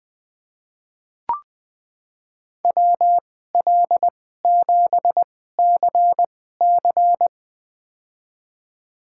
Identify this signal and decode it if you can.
Morse code, heard as a keyed tone, about 20 wpm, spaced as for 14 wpm, 700 Hz. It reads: WL7CC
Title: USA Call Signs - 5 Characters - 2x Character Spacing - 20wpm